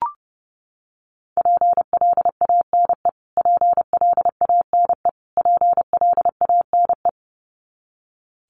Morse code, heard as a keyed tone, about 28 wpm, 700 Hz. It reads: PLANE PLANE PLANE
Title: Top 500 Words Repeated 30wpm